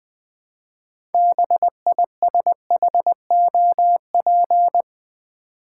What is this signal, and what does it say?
Morse code, heard as a keyed tone, about 20 wpm, 700 Hz. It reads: BISHOP
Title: 6 Letter Words - First 3 Letters Given - 20wpm